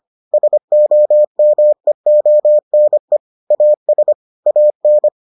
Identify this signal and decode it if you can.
Morse code, heard as a keyed tone, about 25 wpm, 600 Hz. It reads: SOMEONE AS AN